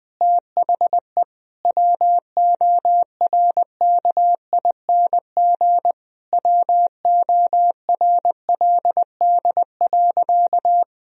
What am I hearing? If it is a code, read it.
Morse code, heard as a keyed tone, about 20 wpm, 700 Hz. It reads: THE WORKING WORLD.
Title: Sentences from Top 1000 Words - Review Part 4 - 20wpm